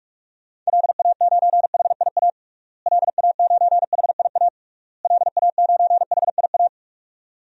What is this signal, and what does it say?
Morse code, heard as a keyed tone, about 46 wpm, 700 Hz. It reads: LA9HIA LA9HIA LA9HIA
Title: DXCC-100 Call Signs - 6 Characters - Repeated 3x - 45wpm